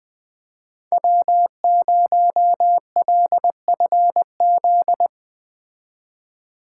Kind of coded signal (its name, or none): Morse code